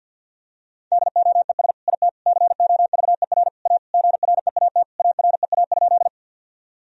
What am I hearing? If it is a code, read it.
Morse code, heard as a keyed tone, about 50 wpm, 700 Hz. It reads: DOES IT COVER A GREAT AREA?
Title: Sentences from Top 1000 Words - Review Part 2 - 50wpm